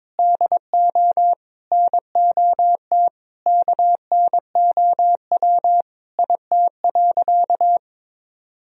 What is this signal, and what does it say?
Morse code, heard as a keyed tone, about 21 wpm, 700 Hz. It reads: DO NOT KNOW IT.
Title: Sentences from Top 1000 Words - Review Part 1 - 22wpm